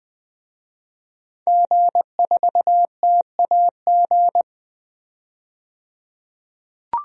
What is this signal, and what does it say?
Morse code, heard as a keyed tone, about 20 wpm, 700 Hz. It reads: G4TAG
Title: DXCC-100 Call Signs - 5 Characters - 20wpm